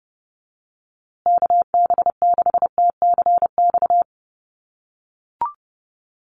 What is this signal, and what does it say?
Morse code, heard as a keyed tone, about 30 wpm, 700 Hz. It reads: KB6TCX
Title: USA Call Signs Repeated 2x 30wpm